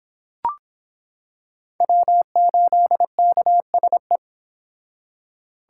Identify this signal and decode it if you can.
Morse code, heard as a keyed tone, about 26 wpm, 700 Hz. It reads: W8KSE